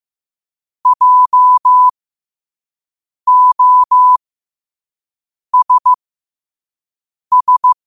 Morse code, heard as a keyed tone, about 15 wpm, spaced as for 5 wpm, 1000 Hz. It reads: JOSS